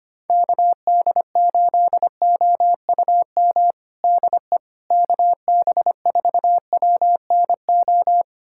Morse code, heard as a keyed tone, about 25 wpm, 700 Hz. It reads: KD8OUM DE KB4WNO
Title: USA Callsign DE Callsign 25wpm